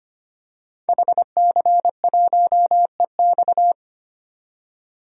Morse code, heard as a keyed tone, about 25 wpm, 700 Hz. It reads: HC1EX
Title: DXCC-100 Call Signs - 5 Characters - 25wpm